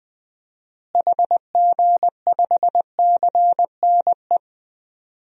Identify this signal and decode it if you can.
Morse code, heard as a keyed tone, about 20 wpm, 700 Hz. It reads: HG5CNE
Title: DXCC-100 Call Signs - 6 Characters - Repeated 2x - 20wpm